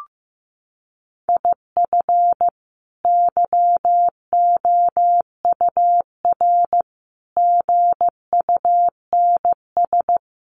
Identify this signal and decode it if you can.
Morse code, heard as a keyed tone, about 14 wpm, 700 Hz. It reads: IF YOUR GUNS
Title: A Sound Of Thunder 15wpm